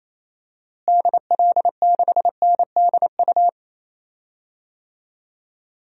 Morse code, heard as a keyed tone, about 28 wpm, 700 Hz. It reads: DL6NDU
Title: DXCC-100 Call Signs - 6 Characters - Repeated 2x - 28wpm